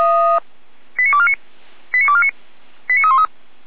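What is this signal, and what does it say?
Five-tone selective call (ZVEI), heard as quick runs of five tones, tones of about 70 ms. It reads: 89249 89249 89313
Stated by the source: ZVEI five-tone call received by SDR, from server of feuerwehr-orlamuende.org (fire brigade Orlamünde).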